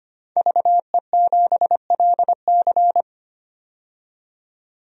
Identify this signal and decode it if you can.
Morse code, heard as a keyed tone, about 25 wpm, 700 Hz. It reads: VE7LC
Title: DXCC-100 Call Signs - 5 Characters - 25wpm